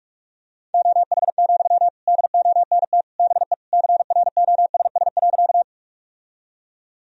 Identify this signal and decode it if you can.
Morse code, heard as a keyed tone, about 43 wpm, 700 Hz. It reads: OH, DONT BE CROSS.
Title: Sentences from Top 1000 Words - Review Part 1 - 45wpm